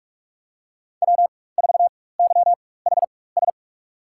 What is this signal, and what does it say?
Morse code, heard as a keyed tone, about 42 wpm, spaced as for 20 wpm, 700 Hz. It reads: W4YHS
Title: USA Call Signs - 4x Character Spacing - 45wpm